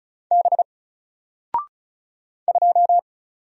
Morse code, heard as a keyed tone, about 36 wpm, 700 Hz. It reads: B 2
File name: Single-Letter-Number-35wpm